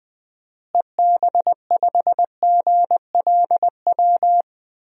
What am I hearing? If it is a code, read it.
Morse code, heard as a keyed tone, about 20 wpm, 700 Hz. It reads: EB5GLW
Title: DXCC-100 Call Signs - 6 Characters - Repeated 3x - 20wpm